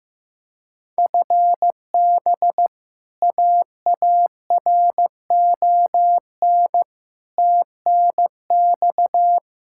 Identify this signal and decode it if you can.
Morse code, heard as a keyed tone, about 15 wpm, 700 Hz. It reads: FB AARON TNX
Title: K1USN SST QSO Practice 15wpm